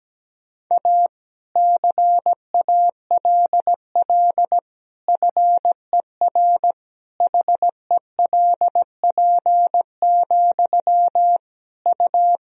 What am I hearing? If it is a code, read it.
Morse code, heard as a keyed tone, about 17 wpm, 700 Hz. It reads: A CALL FER HELP, U
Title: Sentences from Top 300 Words and Abbreviations 17wpm